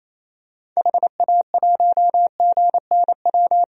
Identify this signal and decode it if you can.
Morse code, heard as a keyed tone, about 28 wpm, 700 Hz. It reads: HA1GNW
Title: DXCC-100 Call Signs - 6 Characters - 28wpm